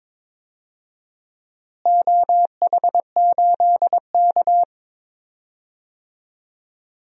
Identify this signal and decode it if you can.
Morse code, heard as a keyed tone, about 21 wpm, 700 Hz. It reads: OH8K
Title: DXCC-100 Call Signs - 4 Characters - Repeated 3x - 22wpm